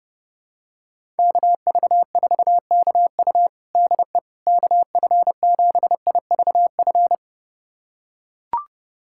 Morse code, heard as a keyed tone, about 30 wpm, 700 Hz. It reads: KV4KU DE KF7IVF
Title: USA Callsign DE Callsign 30wpm